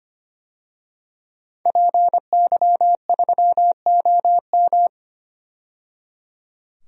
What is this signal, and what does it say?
Morse code, heard as a keyed tone, about 25 wpm, 700 Hz. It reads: PY3OM
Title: DXCC-100 Call Signs - 5 Characters - 25wpm